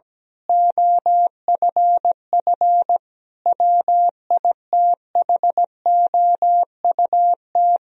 Morse code, heard as a keyed tone, about 17 wpm, 700 Hz. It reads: OFF WITHOUT
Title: Sentences from Top 1000 Words - Review Part 3 - 17wpm